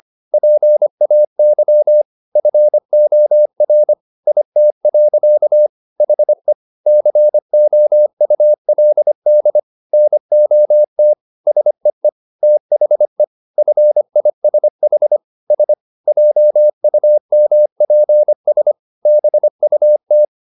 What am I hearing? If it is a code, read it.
Morse code, heard as a keyed tone, about 25 wpm, 600 Hz. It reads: PAY FOR IT. HE COULD NOT SEE THE FISH S JUMPS BUT